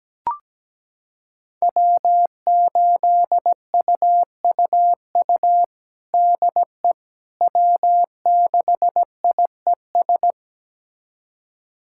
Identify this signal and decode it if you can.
Morse code, heard as a keyed tone, about 17 wpm, 700 Hz. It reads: W8UUU DE W6IES
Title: USA Callsign DE Callsign - 5 Characters - 17wpm